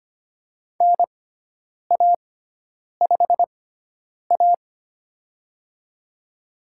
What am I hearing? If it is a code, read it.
Morse code, heard as a keyed tone, about 25 wpm, spaced as for 9 wpm, 700 Hz. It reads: NA5A
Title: USA Call Signs - 4 Characters - 6x Character Spacing - 25wpm